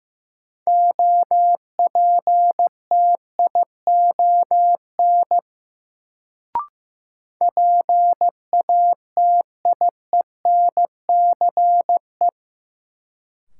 Morse code, heard as a keyed tone, about 15 wpm, 700 Hz. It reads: OPTION PATIENCE
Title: Sentences from Top 1300 Words - New Word Drill - 15wpm